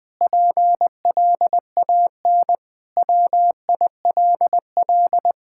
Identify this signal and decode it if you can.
Morse code, heard as a keyed tone, about 21 wpm, 700 Hz. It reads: PLAN WILL
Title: Sentences with 9-Word Phrases - Spoken Beginning - 20wpm